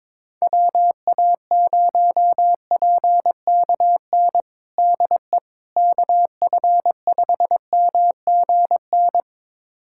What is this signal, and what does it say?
Morse code, heard as a keyed tone, about 22 wpm, 700 Hz. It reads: WA0PKN DE KF5MGN
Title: USA Callsign DE Callsign - 6 Characters - 22wpm